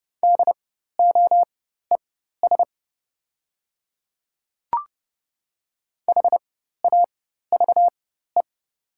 Morse code, heard as a keyed tone, about 29 wpm, spaced as for 14 wpm, 700 Hz. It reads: DOES HAVE
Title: 4x Character Spacing - Top 300 Words - 30wpm